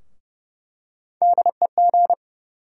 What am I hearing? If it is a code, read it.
Morse code, heard as a keyed tone, about 31 wpm, 700 Hz. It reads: DEG
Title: USA Call Sign Postfixes - 3 Letters 30wpm